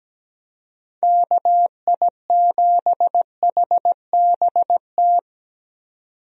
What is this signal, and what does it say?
Morse code, heard as a keyed tone, about 17 wpm, 700 Hz. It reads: KI7HBT
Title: USA Call Signs - 6 Characters - 17wpm